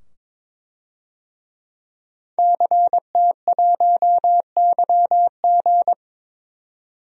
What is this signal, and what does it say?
Morse code, heard as a keyed tone, about 22 wpm, 700 Hz. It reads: CT1YG
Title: DXCC-100 Call Signs - 5 Characters - 22wpm